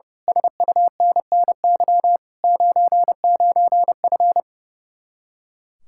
Morse code, heard as a keyed tone, about 30 wpm, 700 Hz. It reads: SUNNY 99F
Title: Speed-Racing Common QSO Phrases Encore - 30wpm